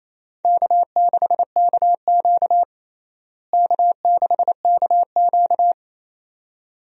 Morse code, heard as a keyed tone, about 28 wpm, 700 Hz. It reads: K6KQ K6KQ
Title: USA Call Signs - 4 Characters - Repeated 2x - 28wpm